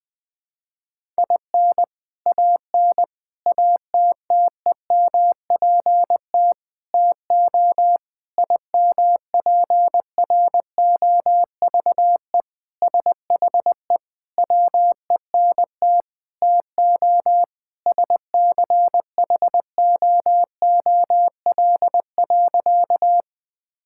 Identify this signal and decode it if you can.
Morse code, heard as a keyed tone, about 20 wpm, 700 Hz. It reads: IN AN ATTEMPT TO IMPROVE SHE WENT TO SCHOOL.
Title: Sentences with 4-Word Phrases - 20wpm